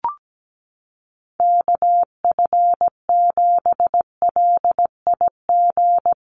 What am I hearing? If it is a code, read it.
Morse code, heard as a keyed tone, about 17 wpm, 700 Hz. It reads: KF7LIG
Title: USA Call Signs 17wpm